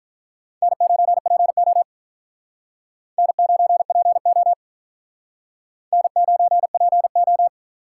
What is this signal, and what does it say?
Morse code, heard as a keyed tone, about 49 wpm, 700 Hz. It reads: N9PO N9PO N9PO